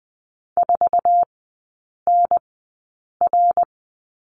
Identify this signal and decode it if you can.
Morse code, heard as a keyed tone, about 20 wpm, spaced as for 8 wpm, 700 Hz. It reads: 4NR
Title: 100 Lesson 17e - Letter u - Three Characters 20wpm